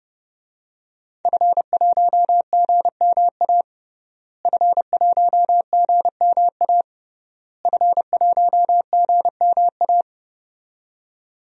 Morse code, heard as a keyed tone, about 30 wpm, 700 Hz. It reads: F1GMA F1GMA F1GMA
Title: DXCC-100 Call Signs - 5 Characters - Repeated 3x - 30wpm